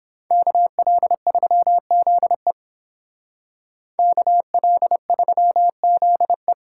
Morse code, heard as a keyed tone, about 31 wpm, 700 Hz. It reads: KL3ZE KL3ZE